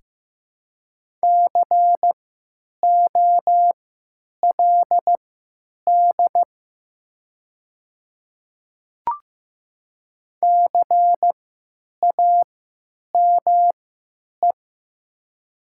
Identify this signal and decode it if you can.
Morse code, heard as a keyed tone, about 15 wpm, spaced as for 9 wpm, 700 Hz. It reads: COLD CAME
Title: 3x Character Spacing - Top 500 Words - 15wpm